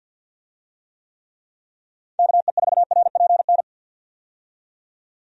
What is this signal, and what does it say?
Morse code, heard as a keyed tone, about 50 wpm, 700 Hz. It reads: KE4RPN
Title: USA Call Signs - 6 Characters - 50wpm